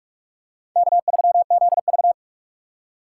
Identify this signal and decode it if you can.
Morse code, heard as a keyed tone, about 44 wpm, 700 Hz. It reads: K3ZV
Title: USA Call Signs - 4 Characters - 6x Character Spacing - 45wpm